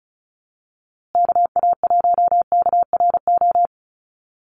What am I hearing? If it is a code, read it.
Morse code, heard as a keyed tone, about 34 wpm, 700 Hz. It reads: KA1KRO
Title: USA Call Signs Repeated 2x 35wpm